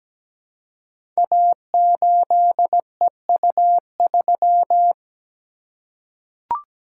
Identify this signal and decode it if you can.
Morse code, heard as a keyed tone, about 17 wpm, 700 Hz. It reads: A8EU3
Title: Code Groups - 5 Letter:Numbers - 17wpm